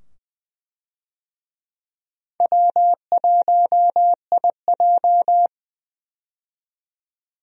Morse code, heard as a keyed tone, about 21 wpm, 700 Hz. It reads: W1IJ
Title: USA Call Signs - 4 Characters - 20wpm